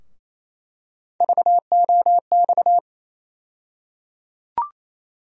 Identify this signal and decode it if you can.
Morse code, heard as a keyed tone, about 28 wpm, 700 Hz. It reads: VOX
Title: USA Call Sign Postfixes - 3 Letters 28wpm